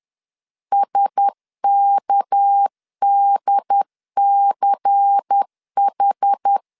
Morse code, heard as a keyed tone, about 10 wpm, 800 Hz. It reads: SKDCH